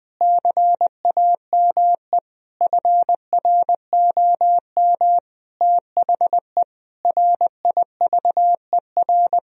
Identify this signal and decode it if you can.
Morse code, heard as a keyed tone, about 21 wpm, 700 Hz. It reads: CAME FROM THE RIVER
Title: Sentences from Top 1000 Words - Review Part 2 - 20wpm